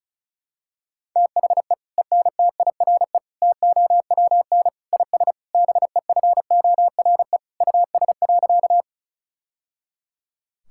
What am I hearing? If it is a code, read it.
Morse code, heard as a keyed tone, about 34 wpm, 700 Hz. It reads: THE ENTIRE TOWN IS BEFORE US.
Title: Sentences from Top 1000 Words - Review Part 5 - 35wpm